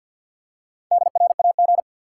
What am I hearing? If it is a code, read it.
Morse code, heard as a keyed tone, about 53 wpm, 700 Hz. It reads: DRAG
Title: Four Letter Words - 50wpm